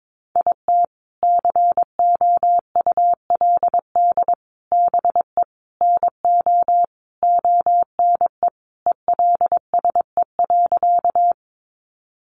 Morse code, heard as a keyed tone, about 22 wpm, 700 Hz. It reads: IT COULD BE NO ONE ELSE.
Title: Sentences from Top 800 Words 22wpm